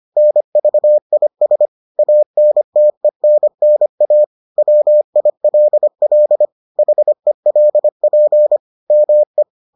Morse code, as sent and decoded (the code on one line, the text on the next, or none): -. ...- .. ... / .- -. - . -. -. .- / .-- .. .-.. .-.. / .... .-.. .--. / -- .
NVIS ANTENNA WILL HELP ME